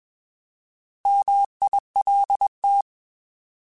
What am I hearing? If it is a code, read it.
Morse code, heard as a keyed tone, about 21 wpm, 800 Hz. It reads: MILT
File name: Alphabetical / Alphabetical mp3s / Ms @20